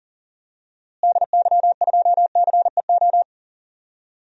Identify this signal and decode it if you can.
Morse code, heard as a keyed tone, about 38 wpm, 700 Hz. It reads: DY2CEO
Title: DXCC-100 Call Signs - 6 Characters - Repeated 3x - 40wpm